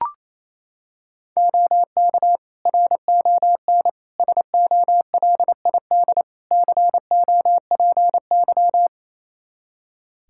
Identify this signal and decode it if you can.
Morse code, heard as a keyed tone, about 28 wpm, 700 Hz. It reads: OK RON SOLID COPY
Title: Practice QSO - Exchange 3 - More Info - 28wpm